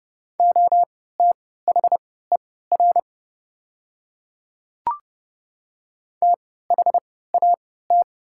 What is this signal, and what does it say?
Morse code, heard as a keyed tone, about 30 wpm, spaced as for 17 wpm, 700 Hz. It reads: OTHER THAT
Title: 3x Character Spacing - Top 100 Words - 30wpm